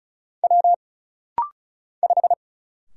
Morse code, heard as a keyed tone, about 34 wpm, 700 Hz. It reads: W 5
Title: Single Letter-Number Alt - Rapid-Fire 35wpm